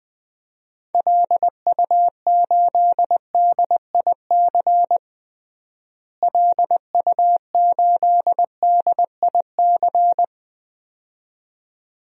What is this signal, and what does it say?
Morse code, heard as a keyed tone, about 20 wpm, 700 Hz. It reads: LU8DIC LU8DIC
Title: DXCC-100 Call Signs - 6 Characters - Repeated 2x - 20wpm